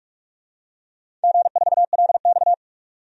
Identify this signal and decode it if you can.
Morse code, heard as a keyed tone, about 46 wpm, 700 Hz. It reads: G4LX
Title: DXCC-100 Call Signs - 4 Characters - 45wpm